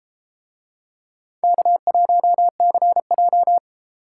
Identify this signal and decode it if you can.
Morse code, heard as a keyed tone, about 29 wpm, 700 Hz. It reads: K1CJ